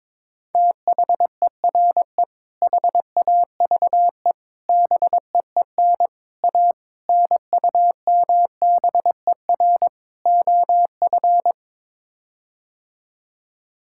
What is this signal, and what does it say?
Morse code, heard as a keyed tone, about 22 wpm, 700 Hz. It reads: THERE HAVE BEEN A NUMBER OF
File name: practice / 22wpm / Sets of 6 Words - 22wpm